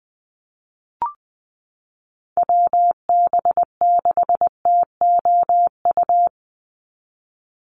Morse code, heard as a keyed tone, about 20 wpm, 700 Hz. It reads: WB6TOU
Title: Call Signs Repeated 2x - 20wpm